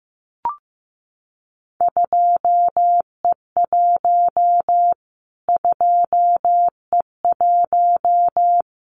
Morse code, heard as a keyed tone, about 15 wpm, 700 Hz. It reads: 2E1 2E1
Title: Callsign Prefixes for Top 25 DXCC - Repeated 3x - 15wpm